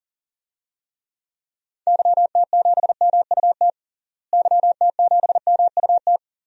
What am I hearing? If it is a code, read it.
Morse code, heard as a keyed tone, about 40 wpm, 700 Hz. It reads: YT7MUT YT7MUT
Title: DXCC-100 Call Signs - 6 Characters - Repeated 3x - 40wpm